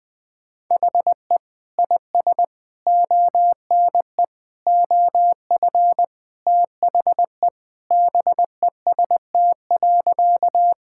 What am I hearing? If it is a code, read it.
Morse code, heard as a keyed tone, about 20 wpm, 700 Hz. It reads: HE IS ONE OF THE BEST.
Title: Sentences with 4-Word Phrases - Spoken Beginning - 20wpm